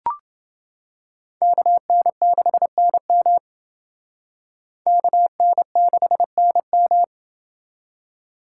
Morse code, heard as a keyed tone, about 28 wpm, 700 Hz. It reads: KN6NM KN6NM